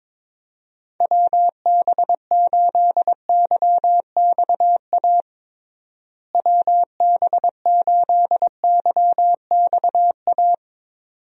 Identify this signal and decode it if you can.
Morse code, heard as a keyed tone, about 22 wpm, 700 Hz. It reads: WB8YXA WB8YXA
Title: USA Call Signs - 6 Characters - Repeated 2x - 22wpm